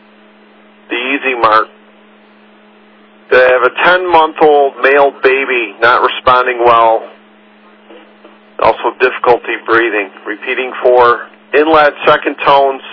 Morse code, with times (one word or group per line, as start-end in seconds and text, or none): none